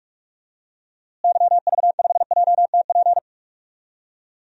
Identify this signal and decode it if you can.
Morse code, heard as a keyed tone, about 41 wpm, 700 Hz. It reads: YV5JTP
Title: DXCC-100 Call Signs - 6 Characters - Repeated 3x - 45wpm